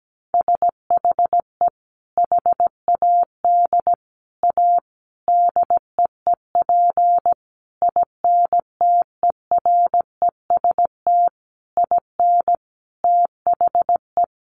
Morse code, heard as a keyed tone, about 17 wpm, 700 Hz. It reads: SHE HAD A DEEP INTEREST IN THE
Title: Sentences from Top 1300 Words 17wpm